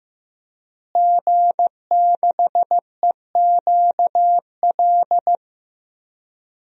Morse code, heard as a keyed tone, about 15 wpm, 700 Hz. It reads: G6EQL
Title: DXCC-100 Call Signs - 5 Characters - Repeated 3x - 15wpm